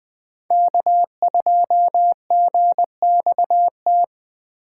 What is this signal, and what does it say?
Morse code, heard as a keyed tone, about 21 wpm, 700 Hz. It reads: K2GXT
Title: USA Call Signs - 2x Character Spacing - 20wpm